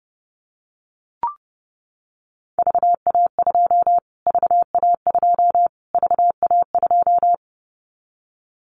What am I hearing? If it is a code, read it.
Morse code, heard as a keyed tone, about 30 wpm, 700 Hz. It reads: VA2 VA2 VA2
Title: Callsign Prefixes for Top 25 DXCC - Repeated 3x - 30wpm